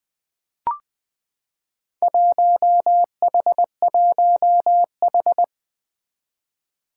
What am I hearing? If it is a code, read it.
Morse code, heard as a keyed tone, about 20 wpm, 700 Hz. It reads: 1H1H
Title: Four Challenging Letter-Number Combinations 20wpm